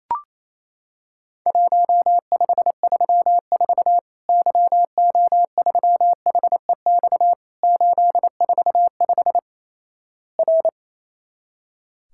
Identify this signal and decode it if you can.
Morse code, heard as a keyed tone, about 28 wpm, 700 Hz. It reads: 1534 YO3HEX 845 R
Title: Worked All Europe DX Contest - QTC Exchange - 28wpm